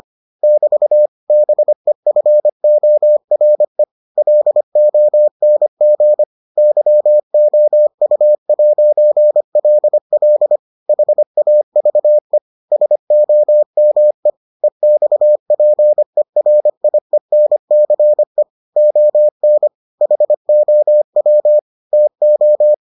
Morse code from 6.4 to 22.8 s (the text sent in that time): YOU'LL HAVE SOME EXPERIENCE ON HOW TO